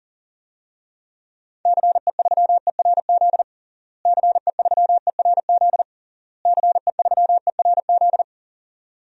Morse code, heard as a keyed tone, about 40 wpm, 700 Hz. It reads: CE3ERZ CE3ERZ CE3ERZ
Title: DXCC-100 Call Signs - 6 Characters - Repeated 3x - 40wpm